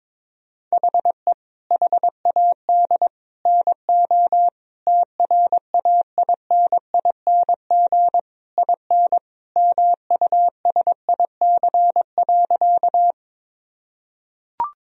Morse code, heard as a keyed tone, about 22 wpm, 700 Hz. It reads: HE HAD NO TRAINING IN MUSIC.
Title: Sentences from Top 1000 Words - Review Part 4 - 22wpm